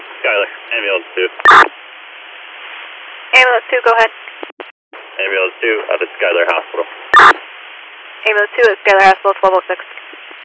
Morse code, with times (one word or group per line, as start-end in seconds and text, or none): none